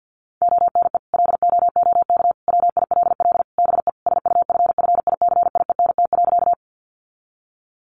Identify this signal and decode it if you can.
Morse code, heard as a keyed tone, about 50 wpm, 700 Hz. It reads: ONE LOOK WILL BE SUFFICIENT.
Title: Sentences from Top 1400 Words 50wpm